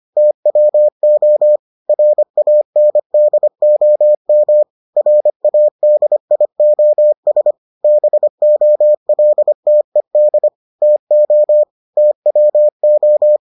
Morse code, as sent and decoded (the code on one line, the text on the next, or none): - .-- --- / .-. .- -. -.. --- -- / .-. .- -.. .. --- ... / -... --- .-.. - . -.. / - --- / - .-- ---
TWO RANDOM RADIOS BOLTED TO TWO